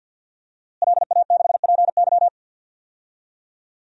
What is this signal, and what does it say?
Morse code, heard as a keyed tone, about 51 wpm, 700 Hz. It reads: LA6PY
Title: DXCC-100 Call Signs - 5 Characters - Repeated 2x - 50wpm